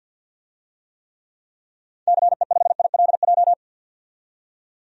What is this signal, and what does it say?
Morse code, heard as a keyed tone, about 52 wpm, 700 Hz. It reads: CE5ILJ
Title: DXCC-100 Call Signs - 6 Characters - 50wpm